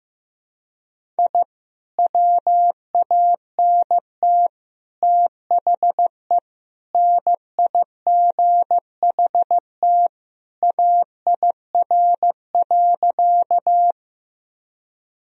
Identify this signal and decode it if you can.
Morse code, heard as a keyed tone, about 14 wpm, 700 Hz. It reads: I WANT THE NIGHT AIR.
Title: Sentences from Top 1000 Words - Review Part 1 - 15wpm